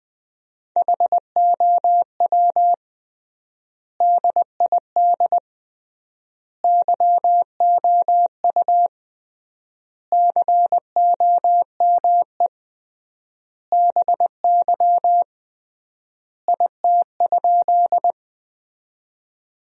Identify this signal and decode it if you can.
Morse code, heard as a keyed tone, about 21 wpm, 700 Hz. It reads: HOW DID YOU COME BY IT?
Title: 3x Word-Spacing - Sentences from Top 100 Words - 20wpm